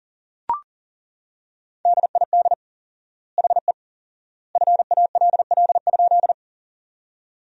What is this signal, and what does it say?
Morse code, heard as a keyed tone, about 40 wpm, 700 Hz. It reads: DID HE FALL?
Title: 4x Word-Spacing - Sentences from Top 500 Words - 40wpm